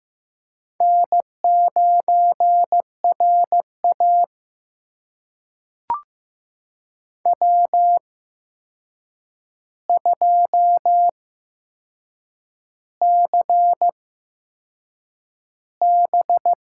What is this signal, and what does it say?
Morse code, heard as a keyed tone, about 15 wpm, spaced as for 6 wpm, 700 Hz. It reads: N9RA W2CB
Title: USA Call Signs - 8x Character Spacing - 15wpm